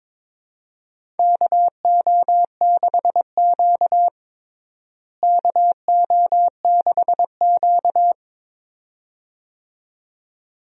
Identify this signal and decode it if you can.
Morse code, heard as a keyed tone, about 22 wpm, 700 Hz. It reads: KO6Q KO6Q
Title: USA Call Signs - 4 Characters - Repeated 2x - 22wpm